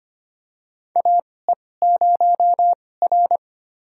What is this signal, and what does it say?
Morse code, heard as a keyed tone, about 25 wpm, spaced as for 18 wpm, 700 Hz. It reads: AE0R
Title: USA Call Signs - 4 Characters - 2x Character Spacing - 25wpm